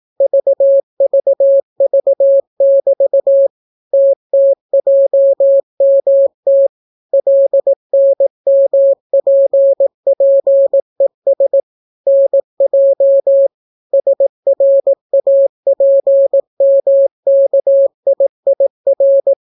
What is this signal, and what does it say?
Morse code, heard as a keyed tone, about 18 wpm, 550 Hz. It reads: VVV<BT> TTJMT LNMPPES NJ SRAPMKIIR